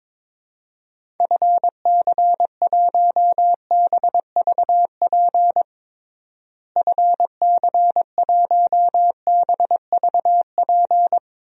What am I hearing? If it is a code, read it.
Morse code, heard as a keyed tone, about 22 wpm, 700 Hz. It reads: FC1BVP FC1BVP
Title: DXCC-100 Call Signs - 6 Characters - Repeated 3x - 22wpm